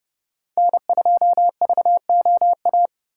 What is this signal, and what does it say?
Morse code, heard as a keyed tone, about 32 wpm, 700 Hz. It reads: N2VOA